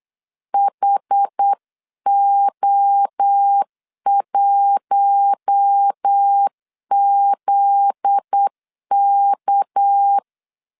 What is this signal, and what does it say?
Morse code, heard as a keyed tone, about 8 wpm, 800 Hz. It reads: HO1ZK
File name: CW / DET / 1 9 1500C S C 8wpm 5653